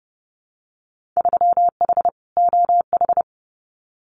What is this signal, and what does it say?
Morse code, heard as a keyed tone, about 30 wpm, 700 Hz. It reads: 3H OH
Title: Winter Field Day Exchanges Encore 30wpm